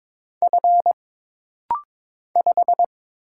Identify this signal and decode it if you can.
Morse code, heard as a keyed tone, about 22 wpm, 700 Hz. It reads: F 5
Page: This rapid-fire timing is geared to push your ICR (Instant Character Recognition) skills.